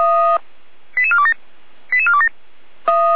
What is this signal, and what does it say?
Five-tone selective call (ZVEI), heard as quick runs of five tones, tones of about 70 ms. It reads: 88427 88427